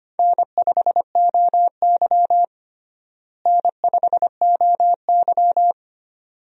Morse code, heard as a keyed tone, about 25 wpm, 700 Hz. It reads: N5OY N5OY
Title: USA Call Signs - 4 Characters - Repeated 2x - 25wpm